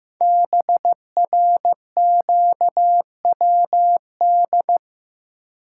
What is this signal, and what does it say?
Morse code, heard as a keyed tone, about 15 wpm, 700 Hz. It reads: BRQWD
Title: Code Groups - 5 Letters - 15wpm